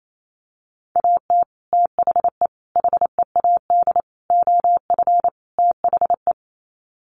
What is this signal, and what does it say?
Morse code, heard as a keyed tone, about 28 wpm, 700 Hz. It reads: AT THE HEAD OF THE
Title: Sets of 5 Words 28wpm